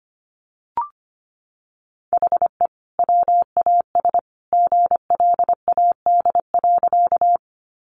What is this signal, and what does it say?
Morse code, heard as a keyed tone, about 25 wpm, 700 Hz. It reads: HE WAS GLAD.
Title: Sentences from Top 800 Words 25wpm